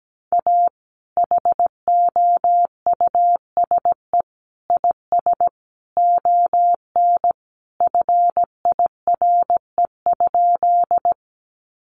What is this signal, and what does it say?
Morse code, heard as a keyed tone, about 17 wpm, 700 Hz. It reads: A HOUSE IS ON FIRE?
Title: Sentences from Top 400 Words 17wpm